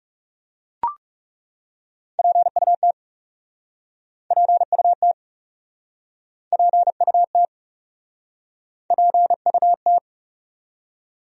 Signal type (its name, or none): Morse code